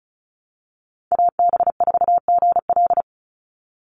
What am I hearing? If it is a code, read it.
Morse code, heard as a keyed tone, about 34 wpm, 700 Hz. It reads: AB4GL
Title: USA Call Signs 35wpm